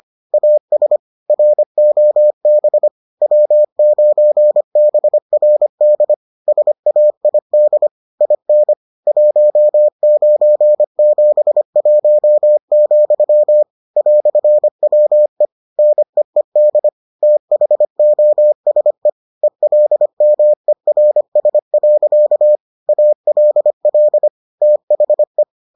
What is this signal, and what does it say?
Morse code, heard as a keyed tone, about 25 wpm, 600 Hz. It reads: AS ROB W9BRD SAID IN 1971, "WE NEED THOSE ELMERS. ALL THE